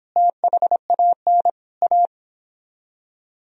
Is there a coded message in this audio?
Yes: Morse code from 0.2 to 2.1 s, heard as a keyed tone, about 29 wpm, 700 Hz. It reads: THAN A